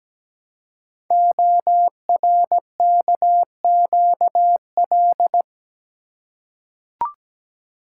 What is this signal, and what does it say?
Morse code, heard as a keyed tone, about 17 wpm, 700 Hz. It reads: ORKQL
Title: Code Groups - 5 Letters - 17wpm